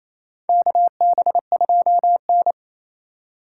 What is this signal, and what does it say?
Morse code, heard as a keyed tone, about 28 wpm, 700 Hz. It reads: KB2N
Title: USA Call Signs - 4 Characters - 6x Character Spacing - 28wpm